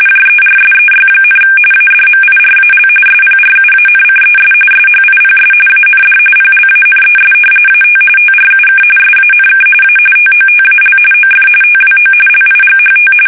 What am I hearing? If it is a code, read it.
RTTY, about 45.45 baud. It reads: JERSEY, ADMIRAL SPRUANCES SHIPS AND PLANES AT TRUK SANK A TOTAL OF TWO LIGHT
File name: fsk850 / battleshipnj